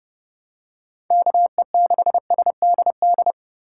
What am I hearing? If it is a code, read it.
Morse code, heard as a keyed tone, about 30 wpm, 700 Hz. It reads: KE6SDD
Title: USA Call Signs - 6 Characters - 3x Character Spacing - 30wpm